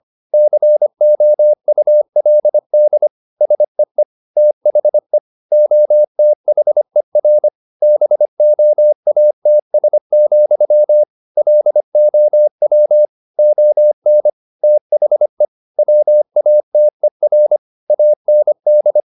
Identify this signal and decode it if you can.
Morse code, heard as a keyed tone, about 25 wpm, 600 Hz. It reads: COULD SEE THE OTHER BOATS, LOW ON THE WATER AND
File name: the old man and the sea part 1 25wpm